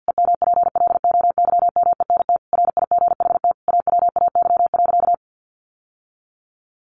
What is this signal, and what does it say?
Morse code, heard as a keyed tone, about 50 wpm, 700 Hz. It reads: EMPLOYMENT RIGHT AWAY.
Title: Sentences from Top 1500 Words 50wpm